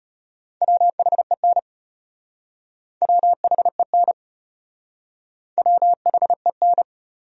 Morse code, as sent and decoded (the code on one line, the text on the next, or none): .-- .... -. / .-- .... -. / .-- .... -.
WHEN WHEN WHEN